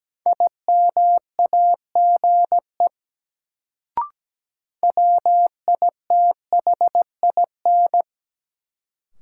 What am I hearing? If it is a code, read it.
Morse code, heard as a keyed tone, about 17 wpm, 700 Hz. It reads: IMAGE WITHIN
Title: Sentences from Top 700 Words - New Word Drill - 17wpm